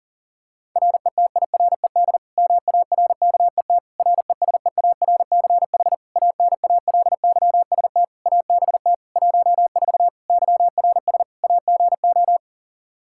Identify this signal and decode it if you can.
Morse code, heard as a keyed tone, about 37 wpm, 700 Hz. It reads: RETIRED MARKET RESEARCH ANALYST ABT 14 YRS AGO
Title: Practice QSO - Exchange 3 - More Info - 40wpm